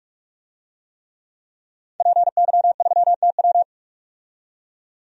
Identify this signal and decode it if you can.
Morse code, heard as a keyed tone, about 45 wpm, 700 Hz. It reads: PY3TW
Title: DXCC-100 Call Signs - 5 Characters - 45wpm